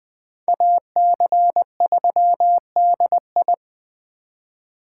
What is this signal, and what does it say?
Morse code, heard as a keyed tone, about 21 wpm, 700 Hz. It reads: AC3DI